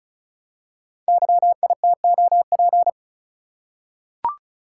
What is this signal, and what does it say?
Morse code, heard as a keyed tone, about 36 wpm, 690 Hz. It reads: YITOP
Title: Code Groups - 5 Letters - 35wpm